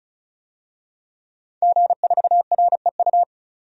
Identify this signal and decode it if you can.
Morse code, heard as a keyed tone, about 34 wpm, 700 Hz. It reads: G4REU
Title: DXCC-100 Call Signs - 5 Characters - 35wpm